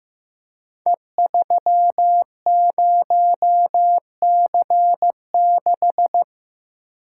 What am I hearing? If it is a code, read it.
Morse code, heard as a keyed tone, about 15 wpm, 700 Hz. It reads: E30C6
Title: Code Groups - 5 Letter:Numbers - 15wpm